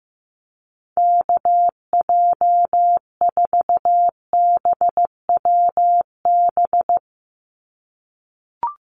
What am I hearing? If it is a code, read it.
Morse code, heard as a keyed tone, about 15 wpm, 700 Hz. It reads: KJ4BWB
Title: USA Call Signs 15wpm